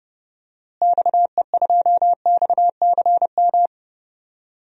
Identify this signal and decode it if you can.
Morse code, heard as a keyed tone, about 30 wpm, 700 Hz. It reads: XE2XCM
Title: DXCC-100 Call Signs - 6 Characters - Repeated 3x - 30wpm